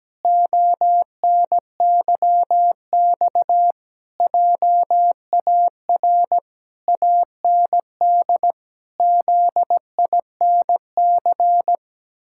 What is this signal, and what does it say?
Morse code, heard as a keyed tone, about 17 wpm, 700 Hz. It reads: ONYX JAR AND ZINC
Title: Pangram Challenge v2 - 17wpm